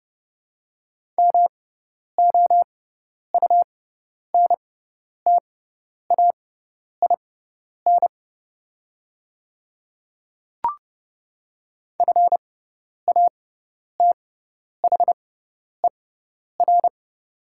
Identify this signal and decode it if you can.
Morse code, heard as a keyed tone, about 30 wpm, spaced as for 10 wpm, 690 Hz. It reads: MOUNTAIN FATHER
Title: 6x Character Spacing - Top 500 Words - 30wpm